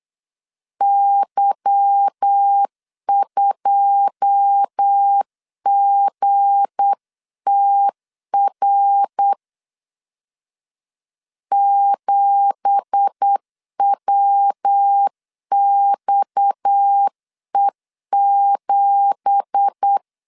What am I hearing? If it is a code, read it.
Morse code, heard as a keyed tone, about 8 wpm, 800 Hz. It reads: Y2GTR 7WXE7
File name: CW / DET / 1 9 1500C S C 8wpm 5653